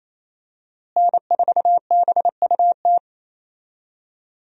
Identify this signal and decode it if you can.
Morse code, heard as a keyed tone, about 28 wpm, 700 Hz. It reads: N4BUT
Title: USA Call Signs - 5 Characters - Repeated 3x - 28wpm